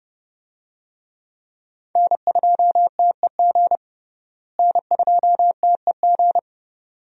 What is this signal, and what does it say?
Morse code, heard as a keyed tone, about 29 wpm, 700 Hz. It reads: N2TEG N2TEG
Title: USA Call Signs - 5 Characters - Repeated 2x - 30wpm